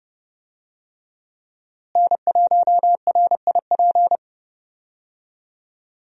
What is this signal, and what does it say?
Morse code, heard as a keyed tone, about 30 wpm, 700 Hz. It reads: N1RIP
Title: USA Call Signs - 5 Characters - Repeated 3x - 30wpm